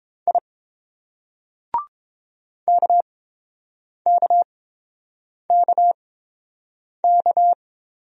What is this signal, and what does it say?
Morse code, heard as a keyed tone, about 27 wpm, spaced as for 7 wpm, 700 Hz. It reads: I KKKK
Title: Speed-Racing Single Letters 22wpm